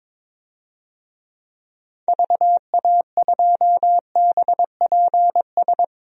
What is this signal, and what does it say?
Morse code, heard as a keyed tone, about 22 wpm, 700 Hz. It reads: VA2BPS
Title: DXCC-100 Call Signs - 6 Characters - Repeated 2x - 22wpm